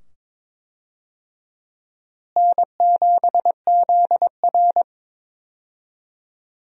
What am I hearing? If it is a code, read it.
Morse code, heard as a keyed tone, about 22 wpm, 700 Hz. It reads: N7ZR
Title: USA Call Signs - 4 Characters - Repeated 2x - 22wpm